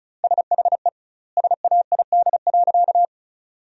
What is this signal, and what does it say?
Morse code, heard as a keyed tone, about 36 wpm, 700 Hz. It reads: SHE SAID.
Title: 2x Word-Spacing - Sentences from Top 100 Words - 35wpm